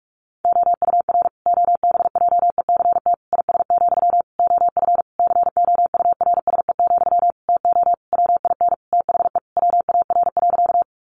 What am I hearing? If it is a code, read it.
Morse code, heard as a keyed tone, about 44 wpm, 700 Hz. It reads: OUR OBJECT IS, OF COURSE, TO WIN THE WAR.
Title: Sentences from Top 1500 Words 45wpm